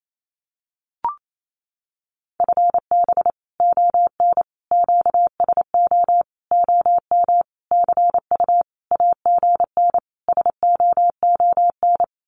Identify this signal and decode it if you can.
Morse code, heard as a keyed tone, about 28 wpm, 700 Hz. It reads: FB ON QSO OM CU AGN SOON
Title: Common QSO Phrases 28wpm